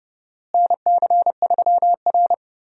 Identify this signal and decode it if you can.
Morse code, heard as a keyed tone, about 31 wpm, 700 Hz. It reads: NC3R
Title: USA Call Signs - 4 Characters - Repeated 2x - 30wpm